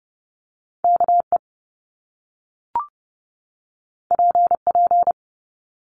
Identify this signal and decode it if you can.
Morse code, heard as a keyed tone, about 30 wpm, 700 Hz. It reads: KE PP
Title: Top Two Letter Combinations in Words 30wpm